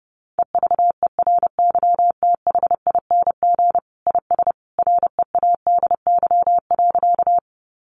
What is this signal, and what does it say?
Morse code, heard as a keyed tone, about 30 wpm, 700 Hz. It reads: EVERYTHING IS READY.
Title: Sentences from Top 700 Words 30wpm